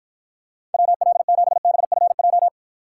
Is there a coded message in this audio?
Yes: Morse code from 0.7 to 2.5 s, heard as a keyed tone, about 52 wpm, 700 Hz. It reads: WL7BFJ